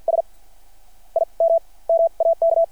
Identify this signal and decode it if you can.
Morse code, heard as a keyed tone, about 50 wpm, spaced as for 21 wpm, 660 Hz. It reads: SIM MAK
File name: archived / CRSnet